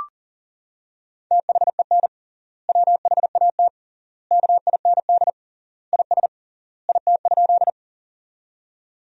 Morse code, heard as a keyed tone, about 40 wpm, 700 Hz. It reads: THEN WHAT KIND IS IT?
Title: 3x Word-Spacing - Sentences from Top 200 Words - 40wpm